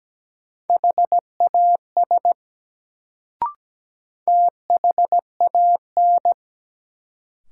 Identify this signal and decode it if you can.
Morse code, heard as a keyed tone, about 17 wpm, 700 Hz. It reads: HAS THAN
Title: Sentences from Top 100 Words - New Word Drill - 17wpm